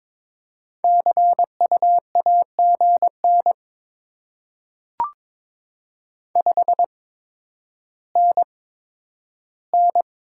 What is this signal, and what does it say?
Morse code, heard as a keyed tone, about 22 wpm, spaced as for 11 wpm, 700 Hz. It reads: CU AGN 5NN
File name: Top 20 QSO Elements - 8x Character Spacing - 22wpm